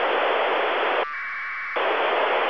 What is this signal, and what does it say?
Packet radio (AX.25, as in APRS).